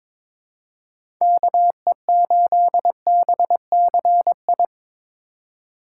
Morse code, heard as a keyed tone, about 22 wpm, 700 Hz. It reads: KE8BCI
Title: USA Call Signs - 6 Characters - 2x Character Spacing - 22wpm